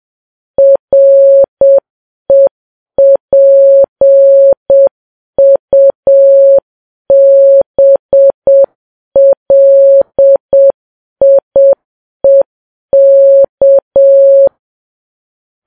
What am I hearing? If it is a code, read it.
Morse code, heard as a keyed tone, about 7 wpm, 550 Hz. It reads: REPUBLIEK